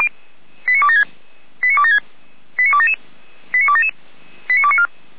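Five-tone selective call (ZVEI), heard as quick runs of five tones, tones of about 80 ms. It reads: 89276 89276 89277 89277 89294
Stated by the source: ZVEI five-tone selective call 89265 received by SDR.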